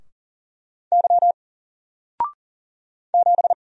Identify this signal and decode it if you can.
Morse code, heard as a keyed tone, about 39 wpm, 700 Hz. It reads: Y 7
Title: Single Letter-Number Alt 40wpm